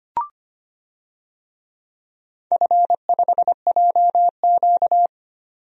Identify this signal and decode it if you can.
Morse code, heard as a keyed tone, about 25 wpm, 700 Hz. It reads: F5JQ
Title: DXCC-100 Call Signs - 4 Characters - 25wpm